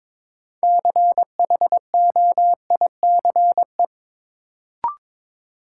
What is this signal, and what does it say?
Morse code, heard as a keyed tone, about 22 wpm, 700 Hz. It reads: CHOICE
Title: Sentences from Top 900 Words - New Word Drill - 22wpm